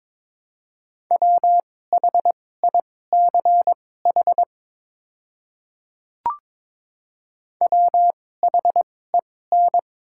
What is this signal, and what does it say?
Morse code, heard as a keyed tone, about 22 wpm, spaced as for 16 wpm, 700 Hz. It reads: WHICH WHEN
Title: 2x Character Spacing - Top 100 Words - 22wpm